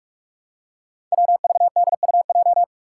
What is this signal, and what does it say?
Morse code, heard as a keyed tone, about 47 wpm, 700 Hz. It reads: WVDUJ
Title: Five Challenging Letter-Number Combinations 45wpm